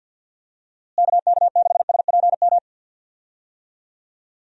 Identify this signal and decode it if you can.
Morse code, heard as a keyed tone, about 51 wpm, 700 Hz. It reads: KK6SPM
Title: USA Call Signs - 6 Characters - Repeated 2x - 50wpm